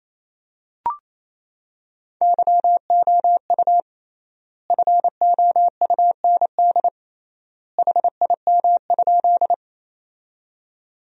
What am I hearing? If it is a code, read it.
Morse code, heard as a keyed tone, about 28 wpm, 700 Hz. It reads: YOU FOUND HIM?
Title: 3x Word-Spacing - Sentences from Top 300 Words - 28wpm